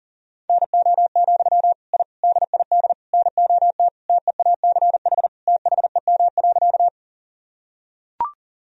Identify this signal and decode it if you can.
Morse code, heard as a keyed tone, about 40 wpm, 700 Hz. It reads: NO, I DID NOT TEACH THEM.
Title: Sentences from Top 1000 Words - Review Part 2 - 40wpm